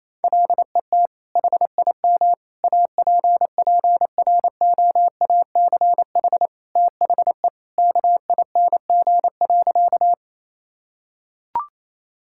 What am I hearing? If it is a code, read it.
Morse code, heard as a keyed tone, about 28 wpm, 700 Hz. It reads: LET HIM APPROACH THE KING.
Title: Sentences from Top 1000 Words - Review Part 6 - 28wpm